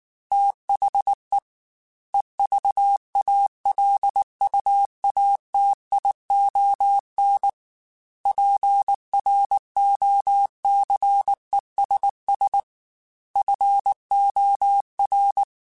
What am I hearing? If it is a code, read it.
Morse code, heard as a keyed tone, about 19 wpm, 800 Hz. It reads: THE EVALUATION PROCESS FOR
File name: Technical Subjects / Technical Subjects-mp3s / AND Gate @18